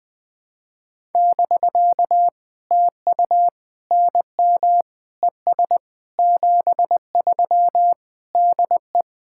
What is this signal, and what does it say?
Morse code, heard as a keyed tone, about 20 wpm, 700 Hz. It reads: <BK> TU NM ES 73 DE
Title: POTA QSO Practice - 20wpm